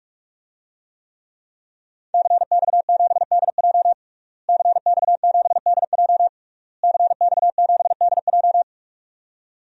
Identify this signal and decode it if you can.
Morse code, heard as a keyed tone, about 41 wpm, 700 Hz. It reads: CX7DJ CX7DJ CX7DJ
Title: DXCC-100 Call Signs - 5 Characters - Repeated 3x - 45wpm